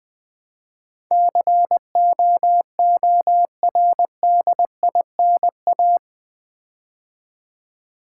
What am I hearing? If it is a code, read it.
Morse code, heard as a keyed tone, about 20 wpm, 700 Hz. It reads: COORDINA
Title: Build A Word - 4 Letters At A Time - 10-Letter Words - 20wpm